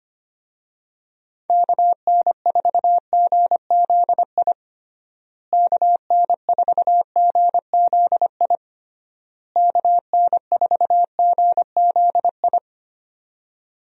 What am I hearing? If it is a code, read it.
Morse code, heard as a keyed tone, about 25 wpm, 700 Hz. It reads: KN4GZI KN4GZI KN4GZI
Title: USA Call Signs - 6 Characters - Repeated 3x - 25wpm